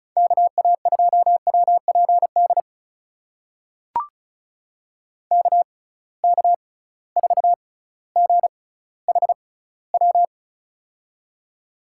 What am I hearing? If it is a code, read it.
Morse code, heard as a keyed tone, about 36 wpm, spaced as for 18 wpm, 700 Hz. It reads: KA2WPD KK4GHW
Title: USA Call Signs - 6x Character Spacing - 35wpm